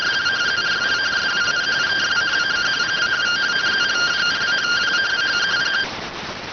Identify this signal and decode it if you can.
Two-tone teleprinter signal, about 170 Hz shift.